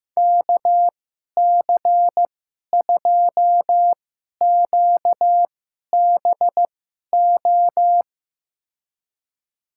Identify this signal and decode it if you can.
Morse code, heard as a keyed tone, about 15 wpm, spaced as for 11 wpm, 700 Hz. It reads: KC2QBO
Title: USA Call Signs - 6 Characters - 2x Character Spacing - 15wpm